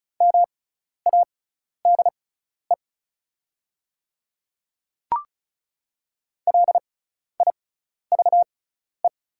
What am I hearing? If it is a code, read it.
Morse code, heard as a keyed tone, about 36 wpm, spaced as for 12 wpm, 700 Hz. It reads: MADE LIVE
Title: 6x Character Spacing - Top 200 Words - 35wpm